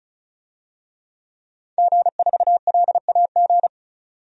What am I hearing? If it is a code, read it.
Morse code, heard as a keyed tone, about 34 wpm, 700 Hz. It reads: G4LAG
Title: DXCC-100 Call Signs - 5 Characters - 35wpm